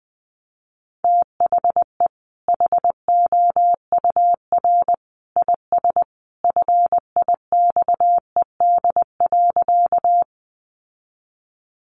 Morse code, heard as a keyed tone, about 20 wpm, 700 Hz. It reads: THE HOUR IS FIXED.
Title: Sentences from Top 1100 Words 20wpm